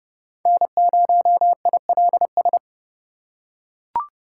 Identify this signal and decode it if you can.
Morse code, heard as a keyed tone, about 30 wpm, 700 Hz. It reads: N0ILS
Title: USA Call Signs - 5 Characters - 3x Character Spacing - 30wpm